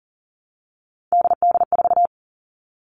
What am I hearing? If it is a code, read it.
Morse code, heard as a keyed tone, about 38 wpm, 700 Hz. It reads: DD4
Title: Three Challenging Letter-Number Combinations 40wpm